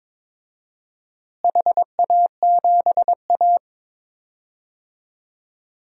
Morse code, heard as a keyed tone, about 22 wpm, 700 Hz. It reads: HA7A
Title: DXCC-100 Call Signs - 4 Characters - Repeated 3x - 22wpm